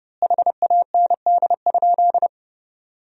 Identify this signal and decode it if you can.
Morse code, heard as a keyed tone, about 32 wpm, 700 Hz. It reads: HAND?
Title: 4x Word-Spacing - Sentences from Top 200 Words - 30wpm